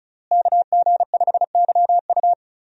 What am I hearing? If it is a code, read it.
Morse code, heard as a keyed tone, about 36 wpm, 700 Hz. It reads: KG5YU
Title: USA Call Signs - 5 Characters - Repeated 2x - 35wpm